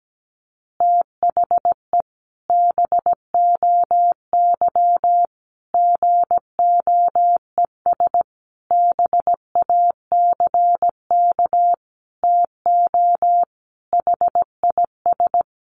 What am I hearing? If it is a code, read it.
Morse code, heard as a keyed tone, about 17 wpm, 700 Hz. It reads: THE BOY GOES BACK TO HIS
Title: Additional Easy Phrases 17wpm